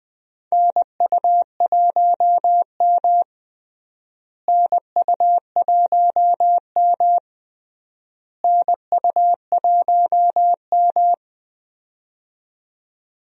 Morse code, heard as a keyed tone, about 21 wpm, 700 Hz. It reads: NU1M NU1M NU1M
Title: USA Call Signs - 4 Characters - Repeated 3x - 20wpm